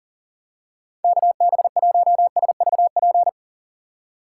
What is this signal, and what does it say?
Morse code, heard as a keyed tone, about 40 wpm, 700 Hz. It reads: KB1SVP
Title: USA Call Signs - 6x Character Spacing - 40wpm